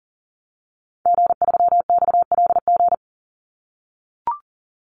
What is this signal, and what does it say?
Morse code, heard as a keyed tone, about 38 wpm, 700 Hz. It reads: G3XLG
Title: Call Signs - 40wpm